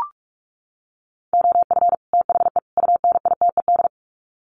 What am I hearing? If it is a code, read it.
Morse code, heard as a keyed tone, about 44 wpm, 700 Hz. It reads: OF THE UNITED
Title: Sets of 3 Words 45wpm